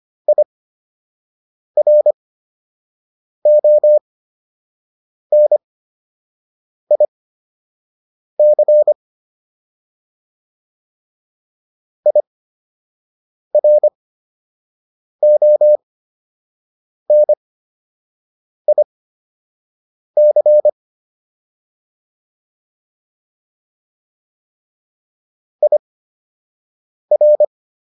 Morse code, heard as a keyed tone, about 25 wpm, spaced as for 6 wpm, 600 Hz. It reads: IRONIC IRONIC IR